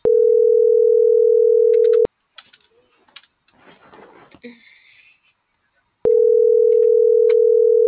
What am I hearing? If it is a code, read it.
Ringback tone (North America), 440 + 480 Hz.
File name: CENTRAL TERRESTRE / out els..laz.elo.mo.24